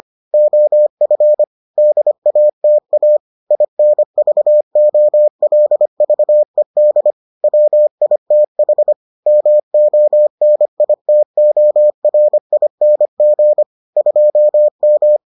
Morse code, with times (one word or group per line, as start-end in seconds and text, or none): 0.3-1.4 OF
1.8-3.2 DATA
3.5-7.1 INVOLVED
7.4-8.9 WITH
9.3-13.6 MONITORING
14.0-15.2 2M